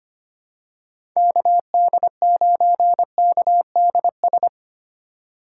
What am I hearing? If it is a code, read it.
Morse code, heard as a keyed tone, about 25 wpm, 700 Hz. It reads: KD9KDS
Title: USA Call Signs - 6 Characters - 2x Character Spacing - 25wpm